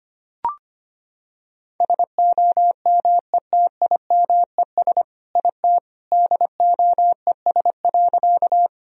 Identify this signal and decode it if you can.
Morse code, heard as a keyed tone, about 25 wpm, 700 Hz. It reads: SOMETIMES IT DOES.
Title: Sentences from Top 1000 Words - Review Part 3 - 25wpm